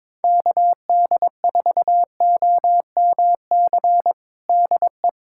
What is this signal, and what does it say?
Morse code, heard as a keyed tone, about 22 wpm, 700 Hz. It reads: KD4OMC DE
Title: USA Callsign DE Callsign - 6 Characters - 22wpm